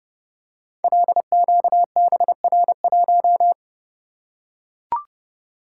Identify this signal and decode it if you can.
Morse code, heard as a keyed tone, about 30 wpm, 700 Hz. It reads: LQBR1
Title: Code Groups - 5 Letter:Numbers - 30wpm